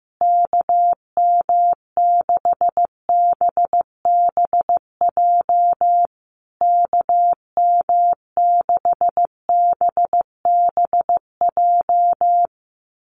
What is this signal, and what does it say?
Morse code, heard as a keyed tone, about 14 wpm, 700 Hz. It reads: KM6BBJ KM6BBJ
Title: USA Call Signs Repeated 3x 15wpm